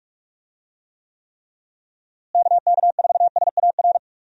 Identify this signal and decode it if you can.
Morse code, heard as a keyed tone, about 45 wpm, 700 Hz. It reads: KK4SAR
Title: USA Call Signs - 6 Characters - Repeated 3x - 45wpm